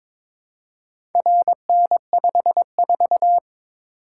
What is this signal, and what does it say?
Morse code, heard as a keyed tone, about 22 wpm, 700 Hz. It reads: RN54
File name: practice / 22wpm / Four Letter-Number Combinations - 22wpm